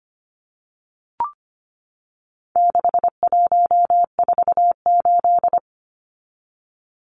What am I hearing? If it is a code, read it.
Morse code, heard as a keyed tone, about 25 wpm, 700 Hz. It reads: 6148
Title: Numbers - 4-Digit 25wpm